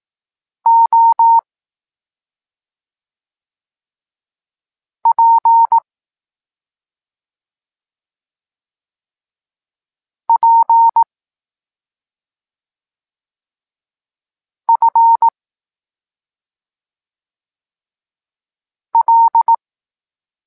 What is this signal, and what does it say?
Morse code, heard as a keyed tone, about 18 wpm, 920 Hz. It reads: O P P F L
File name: Tekst-26 3